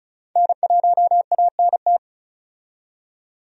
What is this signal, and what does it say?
Morse code, heard as a keyed tone, about 36 wpm, 700 Hz. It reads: N1ANT
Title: USA Call Signs - 5 Characters - 35wpm